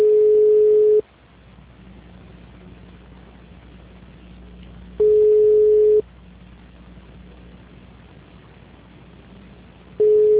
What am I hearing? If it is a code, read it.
Ringback tone (Europe, 425 Hz).